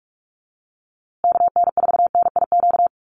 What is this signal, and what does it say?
Morse code, heard as a keyed tone, about 44 wpm, 700 Hz. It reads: KN4NIQ
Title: USA Call Signs Repeated 3x 45wpm